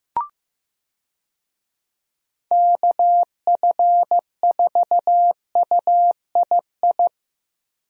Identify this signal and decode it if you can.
Morse code, heard as a keyed tone, about 15 wpm, 700 Hz. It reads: KF4UII
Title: USA Call Signs - 6 Characters - Repeated 2x - 15wpm